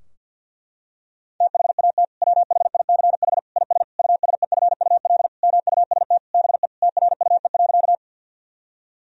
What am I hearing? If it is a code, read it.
Morse code, heard as a keyed tone, about 50 wpm, 700 Hz. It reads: THAT WHICH IS USEFUL MUST BE TRUE.
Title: Sentences from Top 1000 Words - Review Part 4 - 50wpm